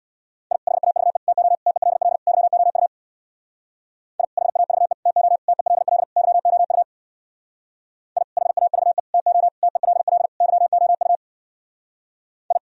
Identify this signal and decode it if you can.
Morse code, heard as a keyed tone, about 64 wpm, 700 Hz. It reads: I HAVE TO TELL YOU I HAVE TO TELL YOU I HAVE TO TELL YOU I